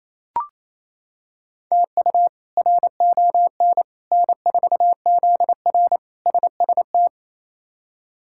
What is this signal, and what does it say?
Morse code, heard as a keyed tone, about 28 wpm, 700 Hz. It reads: TU RON N4ZR SST
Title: K1USN SST QSO Practice - 28wpm